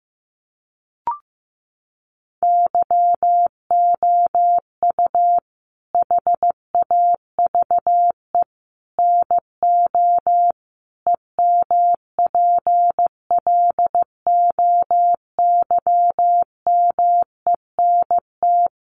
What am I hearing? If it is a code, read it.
Morse code, heard as a keyed tone, about 15 wpm, 700 Hz. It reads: YOU HAVE NO EMPLOYMENT
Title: Sentences from Top 1500 Words 15wpm